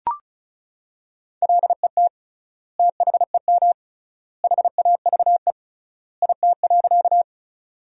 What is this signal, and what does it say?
Morse code, heard as a keyed tone, about 34 wpm, 700 Hz. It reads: LET THEM HAVE IT.
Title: 3x Word-Spacing - Sentences from Top 300 Words - 35wpm